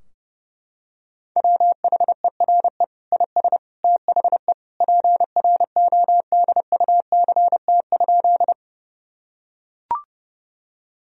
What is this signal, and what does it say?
Morse code, heard as a keyed tone, about 30 wpm, 700 Hz. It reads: WHERE IS THE PRODUCT?
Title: Sentences from Top 1000 Words - Review Part 2 - 30wpm